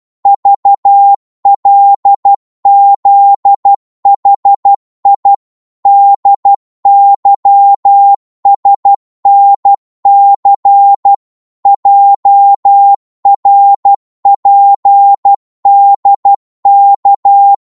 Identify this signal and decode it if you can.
Morse code, heard as a keyed tone, about 12 wpm, 800 Hz. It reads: VLZHI DYSNC JRPDK
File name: cwmp3 / ltr-12wpm